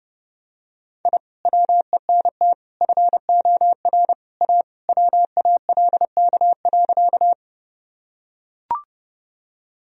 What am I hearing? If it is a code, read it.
Morse code, heard as a keyed tone, about 30 wpm, 700 Hz. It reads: I WENT FOR A WALK.
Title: Sentences from Top 1000 Words - Review Part 1 - 30wpm